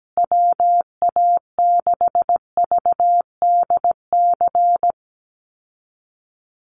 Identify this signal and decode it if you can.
Morse code, heard as a keyed tone, about 17 wpm, 700 Hz. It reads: WA6VDC
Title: USA Call Signs Repeated 2x 17wpm